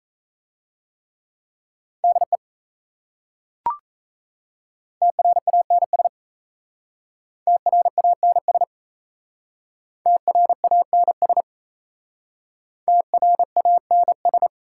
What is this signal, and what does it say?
Morse code, heard as a keyed tone, about 36 wpm, 700 Hz. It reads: DE TRANS TRANS TRANS TRANS